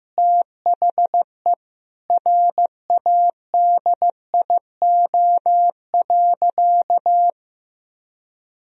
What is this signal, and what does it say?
Morse code, heard as a keyed tone, about 15 wpm, 700 Hz. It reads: THE RADIO.
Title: Sentences from Top 1000 Words - Review Part 3 - 15wpm